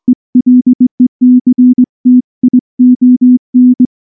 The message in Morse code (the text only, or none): ELECTION